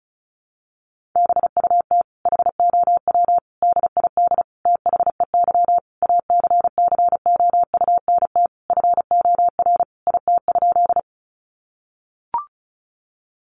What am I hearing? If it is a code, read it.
Morse code, heard as a keyed tone, about 34 wpm, 700 Hz. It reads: BUT HOW DID THEY ACCOUNT FOR IT?
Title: Sentences from Top 900 Words 35wpm